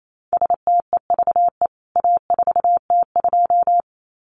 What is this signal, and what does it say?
Morse code, heard as a keyed tone, about 28 wpm, 700 Hz. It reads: STEVE A4T2
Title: CWT Exchange 28wpm